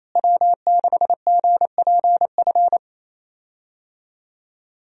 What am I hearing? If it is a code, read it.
Morse code, heard as a keyed tone, about 27 wpm, 700 Hz. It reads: W6GPF
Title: USA Call Signs - 5 Characters - Repeated 3x - 28wpm